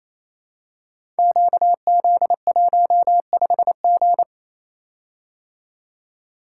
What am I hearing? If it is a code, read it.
Morse code, heard as a keyed tone, about 27 wpm, 700 Hz. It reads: QZ15G
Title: Code Groups - 5 Letter:Numbers - 28wpm